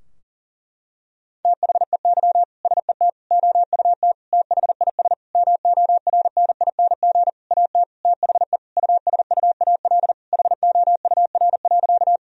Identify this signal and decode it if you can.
Morse code, heard as a keyed tone, about 40 wpm, 700 Hz. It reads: THEY SET OUT THIS MORNING AT THE USUAL HOUR.
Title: Sentences from Top 1000 Words - Review Part 2 - 40wpm